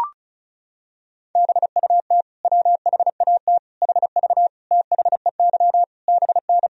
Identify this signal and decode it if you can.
Morse code, heard as a keyed tone, about 36 wpm, 700 Hz. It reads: BUT WHAT HV THEY BN
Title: Sentences from Top 100 Words and Abbreviations 35wpm